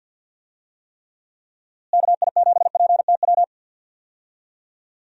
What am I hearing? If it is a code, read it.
Morse code, heard as a keyed tone, about 47 wpm, 700 Hz. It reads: KI7PTW
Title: USA Call Signs - 6 Characters - Repeated 2x - 50wpm